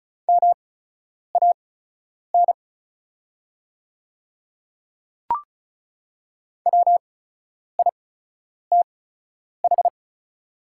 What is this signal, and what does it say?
Morse code, heard as a keyed tone, about 36 wpm, spaced as for 10 wpm, 690 Hz. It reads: MAN WITH
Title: 8x Character Spacing - Top 200 Words - 35wpm